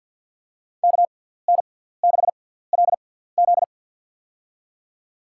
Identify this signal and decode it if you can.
Morse code, heard as a keyed tone, about 49 wpm, spaced as for 17 wpm, 690 Hz. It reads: KN6LZ
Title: USA Call Signs - 5 Characters - 6x Character Spacing - 50wpm